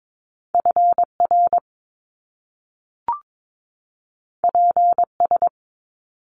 Morse code, heard as a keyed tone, about 22 wpm, 700 Hz. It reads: FR PS
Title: Top Two Letter Combinations in Words 22wpm